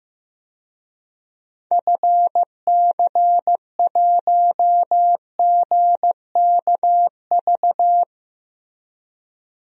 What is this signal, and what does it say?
Morse code, heard as a keyed tone, about 15 wpm, 700 Hz. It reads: FC1GKV
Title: DXCC-100 Call Signs - 6 Characters - Repeated 2x - 15wpm